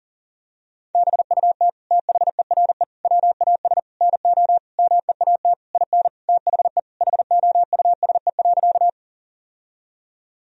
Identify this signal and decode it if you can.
Morse code, heard as a keyed tone, about 40 wpm, 700 Hz. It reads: BUT THERE WAS NO MEAT IN THE HOUSE.
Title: Sentences from Top 1000 Words - Review Part 3 - 40wpm